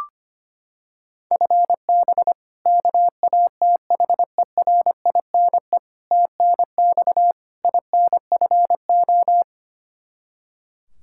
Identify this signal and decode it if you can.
Morse code, heard as a keyed tone, about 25 wpm, 700 Hz. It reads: FB KATHERINE TNX INFO
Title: Practice QSO - Exchange 3 - More Info - 25wpm